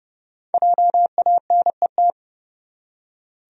Morse code, heard as a keyed tone, about 30 wpm, 700 Hz. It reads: JANET
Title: USA Common Names - Max 5 Letters - 30wpm